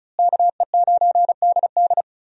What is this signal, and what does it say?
Morse code, heard as a keyed tone, about 36 wpm, 700 Hz. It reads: KE9DD
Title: USA Call Signs - 5 Characters - Repeated 3x - 35wpm